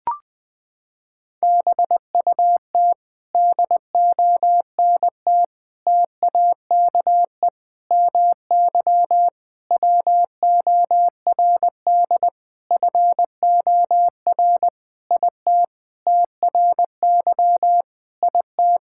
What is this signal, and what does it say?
Morse code, heard as a keyed tone, about 20 wpm, 700 Hz. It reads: BUT DONT TAKE MY WORD FOR IT TRY IT
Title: Sentences with 7-Word Phrases - 20wpm